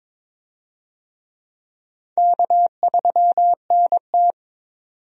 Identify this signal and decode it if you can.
Morse code, heard as a keyed tone, about 23 wpm, 700 Hz. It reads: K3NT
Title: USA Call Signs - 4 Characters - 22wpm